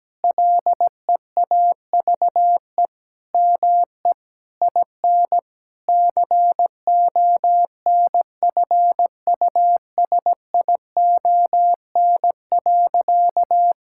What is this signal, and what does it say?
Morse code, heard as a keyed tone, about 17 wpm, 700 Hz. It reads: LEAVE ME IN CONFUSION.
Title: Sentences from Top 1600 Words 17wpm